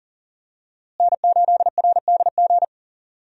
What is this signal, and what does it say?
Morse code, heard as a keyed tone, about 40 wpm, 700 Hz. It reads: N8RDG
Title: USA Call Signs - 5 Characters - Repeated 2x - 40wpm